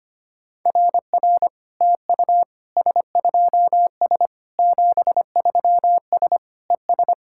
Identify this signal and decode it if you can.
Morse code, heard as a keyed tone, about 24 wpm, 700 Hz. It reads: RR TU S2S 73S ES
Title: SOTA QSO Practice 25wpm